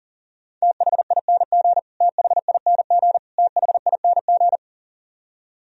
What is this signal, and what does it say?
Morse code, heard as a keyed tone, about 40 wpm, 700 Hz. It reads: THING THING THING
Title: Top 100 Words Repeated 40wpm